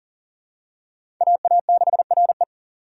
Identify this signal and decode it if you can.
Morse code, heard as a keyed tone, about 40 wpm, 700 Hz. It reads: AA6RE
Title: K1USN SST QSO Practice - 40wpm